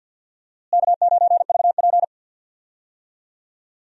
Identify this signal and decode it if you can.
Morse code, heard as a keyed tone, about 49 wpm, 700 Hz. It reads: K9VP
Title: USA Call Signs - 4 Characters - Repeated 2x - 50wpm